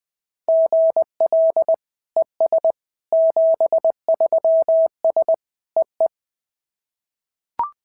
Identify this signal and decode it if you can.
Morse code, heard as a keyed tone, about 19 wpm, 650 Hz. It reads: GL ES 73S EE
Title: SOTA QSO Practice 20wpm